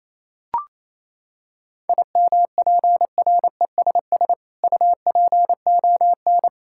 Morse code, heard as a keyed tone, about 28 wpm, 700 Hz. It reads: IMPRESS UPON
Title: Sentences from Top 1600 Words 28wpm